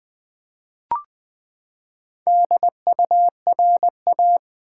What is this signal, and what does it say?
Morse code, heard as a keyed tone, about 20 wpm, 700 Hz. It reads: DURA
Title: Build A Word - 4 Letters At A Time - 8-Letter Words - 20wpm